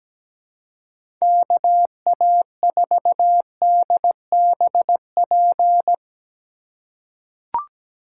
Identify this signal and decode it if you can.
Morse code, heard as a keyed tone, about 17 wpm, 700 Hz. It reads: KA4DBP
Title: USA Call Signs - 6 Characters - 3x Character Spacing - 17wpm